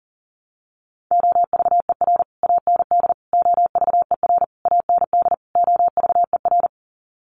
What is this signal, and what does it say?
Morse code, heard as a keyed tone, about 37 wpm, 700 Hz. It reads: OVER AND OVER AND OVER
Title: Sets of 5 Words Encore 40wpm